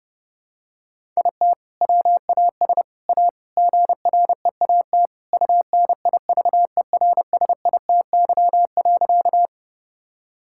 Morse code, heard as a keyed tone, about 30 wpm, 700 Hz. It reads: IT WAS A GREAT UNIVERSITY.
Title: Sentences from Top 1000 Words - Review Part 5 - 30wpm